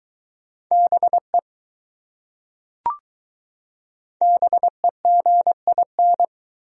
Morse code, heard as a keyed tone, about 20 wpm, 700 Hz. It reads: BE BEGIN